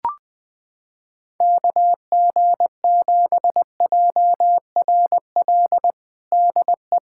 Morse code, heard as a keyed tone, about 20 wpm, 700 Hz. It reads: KG7JRL DE
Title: USA Callsign DE Callsign - 6 Characters - 20wpm